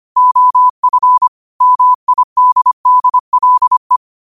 Morse code, heard as a keyed tone, about 25 wpm, 1000 Hz. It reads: OF MIDDLE
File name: qotdfiles / 25wpm